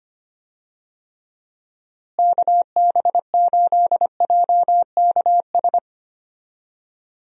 Morse code, heard as a keyed tone, about 25 wpm, 700 Hz. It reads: KB8JKS
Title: USA Call Signs - 6 Characters - Repeated 2x - 25wpm